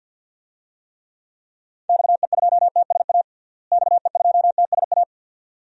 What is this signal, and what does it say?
Morse code, heard as a keyed tone, about 52 wpm, 700 Hz. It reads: XE2TSA XE2TSA
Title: DXCC-100 Call Signs - 6 Characters - Repeated 2x - 50wpm